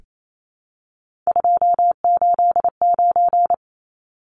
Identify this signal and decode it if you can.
Morse code, heard as a keyed tone, about 28 wpm, 700 Hz. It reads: 289
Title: Numbers - 3-Digit 28wpm